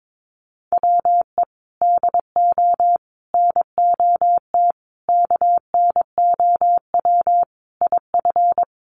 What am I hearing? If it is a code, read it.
Morse code, heard as a keyed tone, about 22 wpm, 700 Hz. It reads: WE DO NOT KNOW IF
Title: Sets of 5 Words Encore2 22wpm